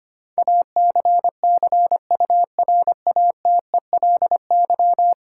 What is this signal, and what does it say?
Morse code, heard as a keyed tone, about 25 wpm, 700 Hz. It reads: ACCURATELY
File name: 10 Letter Words - First 3 Letters Given - 25wpm